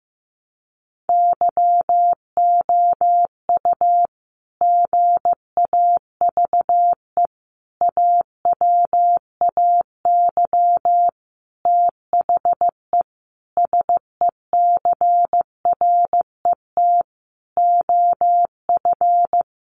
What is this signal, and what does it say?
Morse code, heard as a keyed tone, about 15 wpm, 700 Hz. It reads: YOU GAVE AWAY THE SECRET OF
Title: Sentences from Top 1300 Words 15wpm